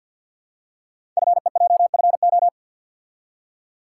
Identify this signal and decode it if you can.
Morse code, heard as a keyed tone, about 47 wpm, 700 Hz. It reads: FEJFO
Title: Code Groups - 5 Letters - 50wpm